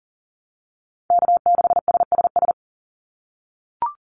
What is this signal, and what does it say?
Morse code, heard as a keyed tone, about 40 wpm, 700 Hz. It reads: K6SSS
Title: Call Signs - 40wpm